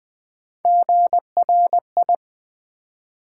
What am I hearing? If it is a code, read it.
Morse code, heard as a keyed tone, about 20 wpm, 700 Hz. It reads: GRI
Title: Build A Word - 1 Letter At A Time - 3 to 5-Letter Words - 20wpm